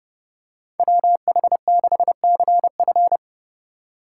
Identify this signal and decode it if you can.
Morse code, heard as a keyed tone, about 31 wpm, 700 Hz. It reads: WH6CF
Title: USA Call Signs - 4x Character Spacing - 30wpm